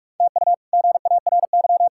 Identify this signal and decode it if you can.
Morse code, heard as a keyed tone, about 46 wpm, 700 Hz. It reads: TU GARY
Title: K1USN SST QSO Practice - 45wpm